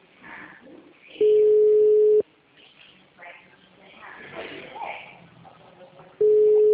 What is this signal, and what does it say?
Ringback tone (Europe, 425 Hz), with speech.